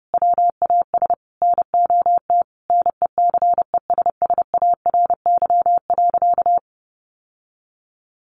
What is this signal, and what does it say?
Morse code, heard as a keyed tone, about 28 wpm, 700 Hz. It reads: WAS NOT NECESSARY.
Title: Sentences from Top 800 Words 30wpm